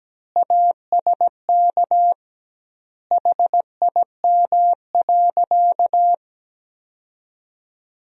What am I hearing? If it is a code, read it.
Morse code, heard as a keyed tone, about 17 wpm, 700 Hz. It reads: ASK HIM.
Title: 2x Word-Spacing - Sentences from Top 300 Words - 17wpm